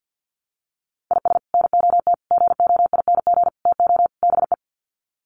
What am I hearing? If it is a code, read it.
Morse code, heard as a keyed tone, about 50 wpm, 700 Hz. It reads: IS NOT GOING TO BE
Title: Sets of 5 Words 50wpm